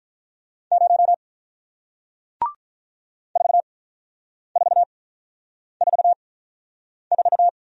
Morse code, heard as a keyed tone, about 48 wpm, spaced as for 9 wpm, 700 Hz. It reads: 0 4444